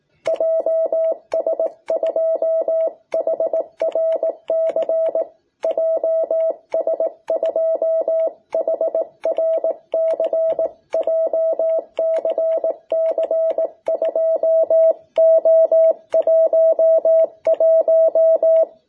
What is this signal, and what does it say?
Morse code, heard as a keyed tone, about 18 wpm, 660 Hz. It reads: JS2HRC JS2HRC JCC2O11
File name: jh8cbh / 2021cw